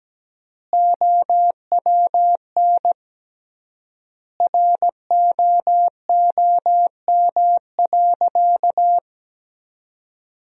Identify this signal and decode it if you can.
Morse code, heard as a keyed tone, about 17 wpm, 700 Hz. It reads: OWN ROOM.
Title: 3x Word-Spacing - Sentences from Top 300 Words - 17wpm